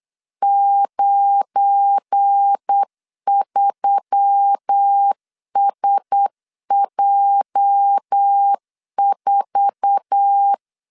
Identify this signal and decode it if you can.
Morse code, heard as a keyed tone, about 8 wpm, 800 Hz. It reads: 93SJ4
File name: LEZIONI / 8 YJ45 / 8 YJ4 8wpm 3246